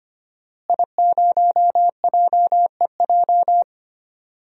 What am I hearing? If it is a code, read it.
Morse code, heard as a keyed tone, about 25 wpm, 700 Hz. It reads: I0JEJ
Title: DXCC-100 Call Signs - 5 Characters - Repeated 2x - 25wpm